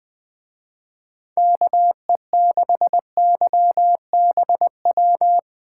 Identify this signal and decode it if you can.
Morse code, heard as a keyed tone, about 22 wpm, 700 Hz. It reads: KE6YBW
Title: Speed Racing - USA 2x3 Call Signs 20wpm